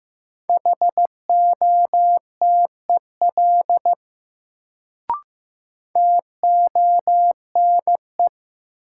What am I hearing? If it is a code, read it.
Morse code, heard as a keyed tone, about 15 wpm, 700 Hz. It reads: HOTEL TONE
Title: Sentences from Top 1600 Words - New Word Drill Alt - 15wpm